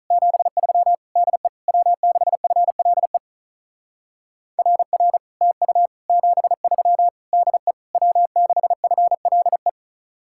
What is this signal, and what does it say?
Morse code, heard as a keyed tone, about 46 wpm, 700 Hz. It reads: 73 DE W6FLE RR TU 73 DE W6FLE